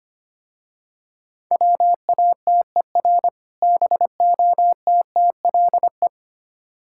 Morse code, heard as a keyed tone, about 27 wpm, 700 Hz. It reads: WATER BOTTLE
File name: Easy Words - Speed-Racing with Hints - 25wpm